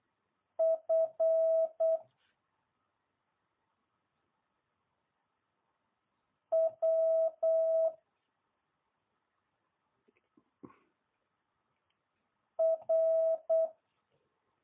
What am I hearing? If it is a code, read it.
Morse code, heard as a keyed tone, about 8 wpm, 650 Hz. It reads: F W R